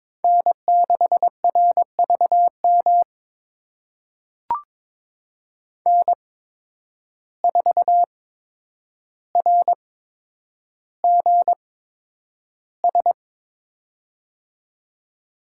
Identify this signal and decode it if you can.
Morse code, heard as a keyed tone, about 22 wpm, spaced as for 9 wpm, 700 Hz. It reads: N6RVM N4RGS
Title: USA Call Signs - 5 Characters - 8x Character Spacing - 22wpm